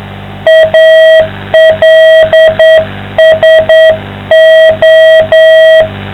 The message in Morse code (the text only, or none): ALSO